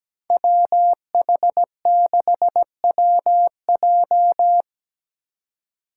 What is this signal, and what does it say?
Morse code, heard as a keyed tone, about 17 wpm, 700 Hz. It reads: WH6WJ
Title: USA Call Signs - 5 Characters - 4x Character Spacing - 17wpm